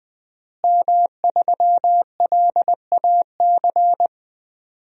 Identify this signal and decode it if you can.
Morse code, heard as a keyed tone, about 20 wpm, 700 Hz. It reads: M3LAC
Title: DXCC-100 Call Signs - 5 Characters - Repeated 3x - 20wpm